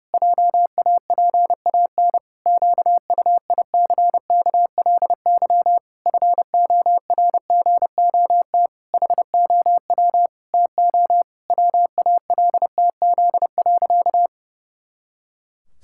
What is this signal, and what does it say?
Morse code, heard as a keyed tone, about 30 wpm, 700 Hz. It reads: JAPAN QUICKLY FORGOT HOW TO WALTZ.
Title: Pangram Challenge v2 - 30wpm